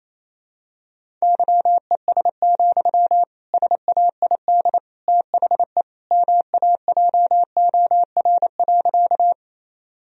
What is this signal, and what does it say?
Morse code, heard as a keyed tone, about 28 wpm, 700 Hz. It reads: YES, SAID THE MAJOR.
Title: Sentences from Top 1000 Words - Review Part 4 - 28wpm